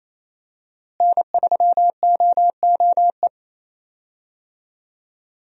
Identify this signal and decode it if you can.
Morse code, heard as a keyed tone, about 28 wpm, 700 Hz. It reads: N3OOE
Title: USA Call Signs - 5 Characters - Repeated 2x - 28wpm